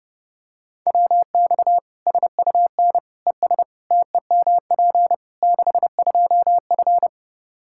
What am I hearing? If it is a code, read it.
Morse code, heard as a keyed tone, about 30 wpm, 700 Hz. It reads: WX SUN ES TEMP 62F
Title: Practice QSO - Exchange 2 - Info - 30wpm